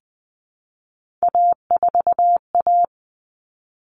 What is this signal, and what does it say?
Morse code, heard as a keyed tone, about 20 wpm, 700 Hz. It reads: A4A
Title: All Cut Numbers - 3-Digit 20wpm